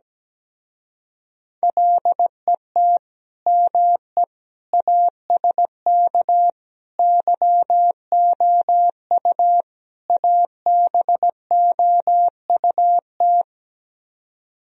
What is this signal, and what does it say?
Morse code, heard as a keyed tone, about 17 wpm, 700 Hz. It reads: LET ME ASK YOU ABOUT
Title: Speed-Racing - Sets of 5 Words 17wpm